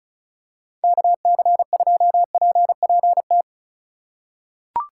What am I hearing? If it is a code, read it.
Morse code, heard as a keyed tone, about 36 wpm, 700 Hz. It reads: KC2PPT
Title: USA Call Signs - 6 Characters - 6x Character Spacing - 35wpm